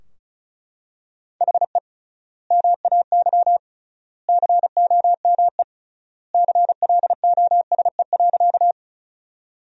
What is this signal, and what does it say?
Morse code, heard as a keyed tone, about 36 wpm, 700 Hz. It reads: HE MAY COME CLOSE.
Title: 3x Word-Spacing - Sentences from Top 300 Words - 35wpm